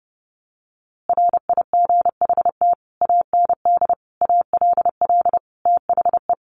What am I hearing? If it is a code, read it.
Morse code, heard as a keyed tone, about 28 wpm, 700 Hz. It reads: RIGHT AND ALL THE
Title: A Sound Of Thunder 30wpm